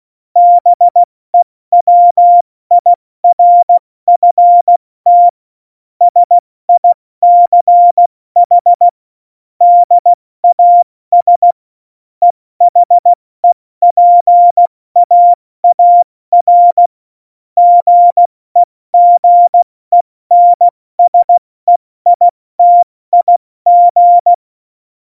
Morse code, heard as a keyed tone, about 16 wpm, 700 Hz. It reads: BEWIRFT SICH DAS EHEPAAR GEGENSEITIG
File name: Morsekurs / Witze / Witz 17.Dezember 2019, 70 F80